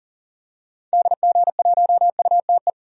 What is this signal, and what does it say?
Morse code, heard as a keyed tone, about 40 wpm, 700 Hz. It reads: DG1UTE
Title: DXCC-100 Call Signs - 6 Characters - 40wpm